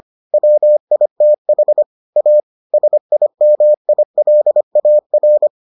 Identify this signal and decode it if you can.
Morse code, heard as a keyed tone, about 25 wpm, 600 Hz. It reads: WITH A SIMILAR